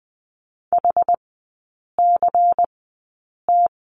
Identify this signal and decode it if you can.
Morse code, heard as a keyed tone, about 20 wpm, spaced as for 8 wpm, 690 Hz. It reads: HCT
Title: 093 Lesson 16e - Letter c - Three Characters 20wpm